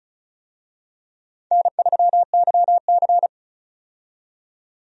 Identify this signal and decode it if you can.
Morse code, heard as a keyed tone, about 34 wpm, 700 Hz. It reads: N3YC
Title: USA Call Signs - 4 Characters - 35wpm